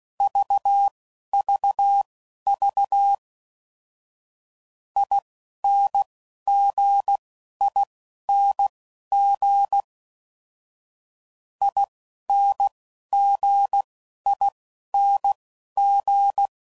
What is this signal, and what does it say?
Morse code, heard as a keyed tone, about 16 wpm, spaced as for 11 wpm, 800 Hz. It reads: VVV INGING INGING
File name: Suffixes / Special INGs / Dble INGs